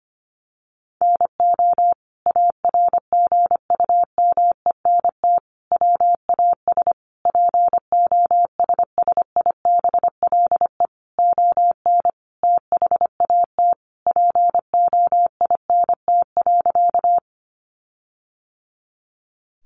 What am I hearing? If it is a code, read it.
Morse code, heard as a keyed tone, about 25 wpm, 700 Hz. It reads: NO ARGUMENT WAS POSSIBLE ON THAT POINT.
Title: Sentences from Top 1400 Words 25wpm